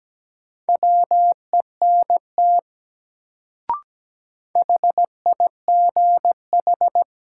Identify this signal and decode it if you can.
Morse code, heard as a keyed tone, about 17 wpm, 700 Hz. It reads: WENT HIGH
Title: Sentences from Top 200 Words - New Word Drill - 17wpm